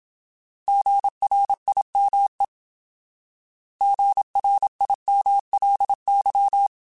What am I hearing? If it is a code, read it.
Morse code, heard as a keyed tone, about 27 wpm, 800 Hz. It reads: GRIME GRIMLY
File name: Alphabetical / Alphabetical mp3s / Gs @25